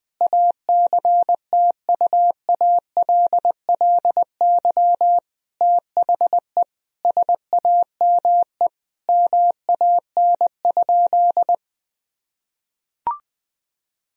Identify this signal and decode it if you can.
Morse code, heard as a keyed tone, about 20 wpm, 700 Hz. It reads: ACTUALLY THE SAME MAN?
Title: Sentences from Top 1000 Words - Review Part 4 - 20wpm